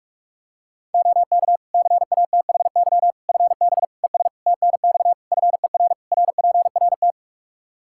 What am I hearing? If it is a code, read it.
Morse code, heard as a keyed tone, about 46 wpm, 700 Hz. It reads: OK CATHY FB ES TNX FER RPRT
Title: K1USN SST QSO Practice 45wpm